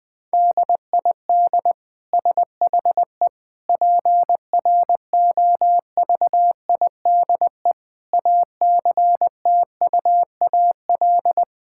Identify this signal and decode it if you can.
Morse code, heard as a keyed tone, about 20 wpm, 700 Hz. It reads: DID SHE PROVIDE ACTUAL
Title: Sentences from Top 1000 Words - Review Part 5 - 20wpm